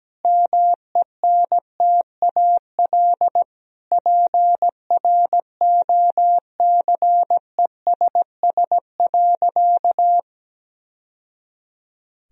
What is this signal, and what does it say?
Morse code, heard as a keyed tone, about 17 wpm, 700 Hz. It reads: MENTAL PROCESS.
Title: Sentences from Top 1000 Words - Review Part 4 - 17wpm